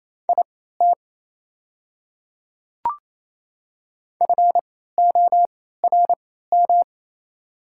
Morse code, heard as a keyed tone, about 28 wpm, spaced as for 16 wpm, 690 Hz. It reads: IT FORM
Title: 3x Character Spacing - Top 200 Words - 28wpm